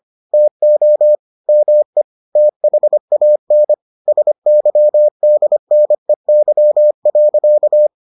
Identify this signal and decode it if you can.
Morse code, heard as a keyed tone, about 25 wpm, 600 Hz. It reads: TO ME THAN SYDNEY.